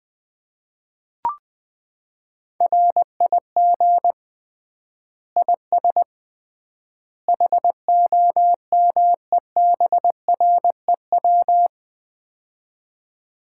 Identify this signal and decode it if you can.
Morse code, heard as a keyed tone, about 20 wpm, 700 Hz. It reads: RIG IS HOMEBREW
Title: Common QSO Phrases - 3x Word-Spacing - 20wpm